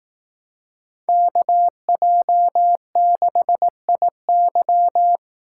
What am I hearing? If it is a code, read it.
Morse code, heard as a keyed tone, about 18 wpm, 700 Hz. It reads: KJ6IY
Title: Speed Racing - USA 2x2 Call Signs 15wpm